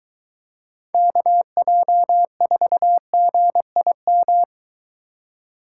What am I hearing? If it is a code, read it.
Morse code, heard as a keyed tone, about 23 wpm, 700 Hz. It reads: KJ4GIM